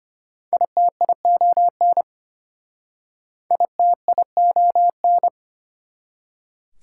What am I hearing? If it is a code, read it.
Morse code, heard as a keyed tone, about 33 wpm, 690 Hz. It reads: ITION ITION